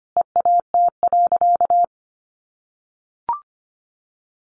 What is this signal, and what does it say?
Morse code, heard as a keyed tone, about 25 wpm, 700 Hz. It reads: EAT.
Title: Sentences from Top 600 Words 25wpm